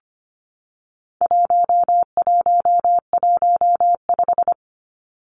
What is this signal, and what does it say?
Morse code, heard as a keyed tone, about 25 wpm, 700 Hz. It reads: 1115
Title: Numbers - 4-Digit 25wpm